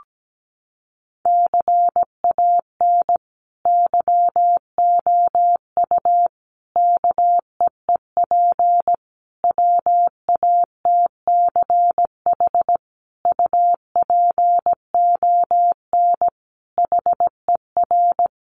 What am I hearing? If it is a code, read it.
Morse code, heard as a keyed tone, about 17 wpm, 700 Hz. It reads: CAN YOU KEEP WATCH UPON HER
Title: Sentences from Top 1100 Words 17wpm